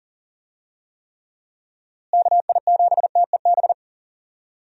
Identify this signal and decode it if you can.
Morse code, heard as a keyed tone, about 40 wpm, 700 Hz. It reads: KI7TEB
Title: USA Call Signs - 6 Characters - Repeated 3x - 40wpm